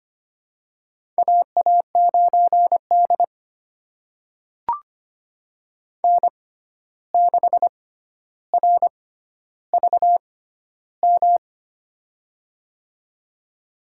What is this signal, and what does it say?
Morse code, heard as a keyed tone, about 25 wpm, spaced as for 12 wpm, 700 Hz. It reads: AA9D N6RVM
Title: USA Call Signs - 6x Character Spacing - 25wpm